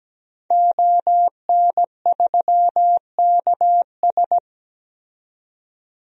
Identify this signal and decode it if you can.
Morse code, heard as a keyed tone, about 17 wpm, 700 Hz. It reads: ON3KS
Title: DXCC-100 Call Signs - 5 Characters - 17wpm